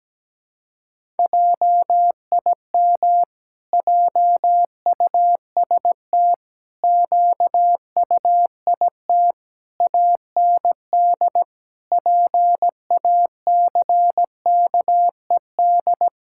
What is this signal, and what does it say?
Morse code, heard as a keyed tone, about 17 wpm, 700 Hz. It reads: JIM JUST QUIT AND PACKED
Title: Pangram Challenge v2 - 17wpm